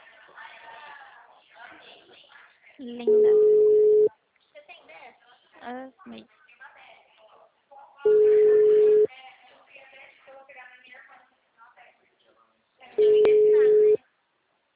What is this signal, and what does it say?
Ringback tone (Europe, 425 Hz), with speech.
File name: out- -2002- - - .599